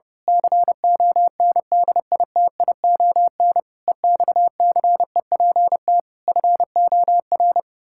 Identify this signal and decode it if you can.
Morse code, heard as a keyed tone, about 28 wpm, 700 Hz. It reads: CONDITION EXCEPT FOR
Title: Pangram Challenge v2 - 30wpm